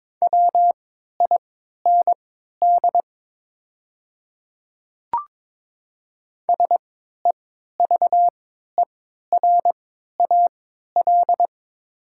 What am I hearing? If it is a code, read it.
Morse code, heard as a keyed tone, about 22 wpm, spaced as for 12 wpm, 700 Hz. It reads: WIND SEVERAL
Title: 3x Character Spacing - Top 500 Words - 22wpm